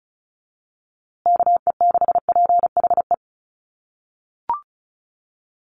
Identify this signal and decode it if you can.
Morse code, heard as a keyed tone, about 34 wpm, 700 Hz. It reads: KE6PHE
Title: USA Call Signs Repeated 3x 35wpm